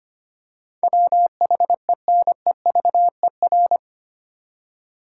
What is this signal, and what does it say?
Morse code, heard as a keyed tone, about 25 wpm, 700 Hz. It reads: WHENEVER
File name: Eight Letter Words - 25wpm